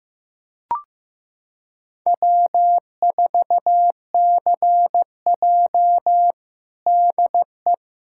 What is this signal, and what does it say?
Morse code, heard as a keyed tone, about 15 wpm, 700 Hz. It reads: W4CJ DE
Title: USA Callsign DE Callsign - 4 Characters - 15wpm